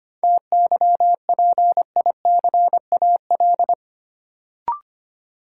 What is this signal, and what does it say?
Morse code, heard as a keyed tone, about 25 wpm, 700 Hz. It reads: TYPICAL
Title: Sentences from Top 1100 Words - New Word Drill Alt - 25wpm